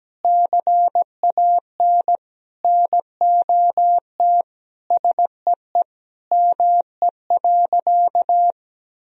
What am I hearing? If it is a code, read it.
Morse code, heard as a keyed tone, about 17 wpm, 700 Hz. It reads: CAN NOT SEE ME.
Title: Sentences from Top 1000 Words - Review Part 3 - 17wpm